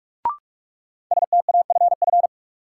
Morse code, heard as a keyed tone, about 45 wpm, 700 Hz. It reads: STAFF
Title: Sentences from Top 1200 Words - New Word Drill Alt - 45wpm